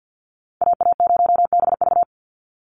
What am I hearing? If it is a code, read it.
Morse code, heard as a keyed tone, about 50 wpm, 700 Hz. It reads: AA0BV
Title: NA SOTA Activator Callsigns 50wpm